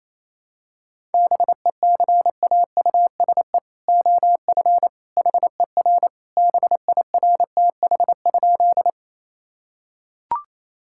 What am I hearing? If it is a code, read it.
Morse code, heard as a keyed tone, about 28 wpm, 700 Hz. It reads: BECAUSE OF HER BIRTH?
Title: Sentences from Top 1000 Words - Review Part 5 - 28wpm